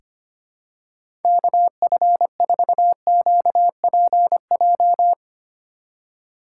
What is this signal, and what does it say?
Morse code, heard as a keyed tone, about 25 wpm, 700 Hz. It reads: KF4QPJ
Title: USA Call Signs - 6 Characters - 8x Character Spacing - 25wpm